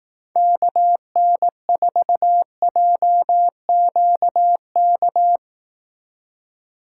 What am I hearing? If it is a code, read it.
Morse code, heard as a keyed tone, about 18 wpm, 700 Hz. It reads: KN4JQK
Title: Speed Racing - USA 2x3 Call Signs 15wpm